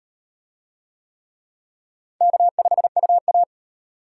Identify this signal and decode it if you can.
Morse code, heard as a keyed tone, about 38 wpm, 700 Hz. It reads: K5UA